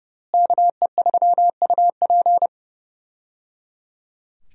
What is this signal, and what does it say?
Morse code, heard as a keyed tone, about 29 wpm, 700 Hz. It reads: KE3UP
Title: USA Call Signs - 5 Characters - Repeated 3x - 30wpm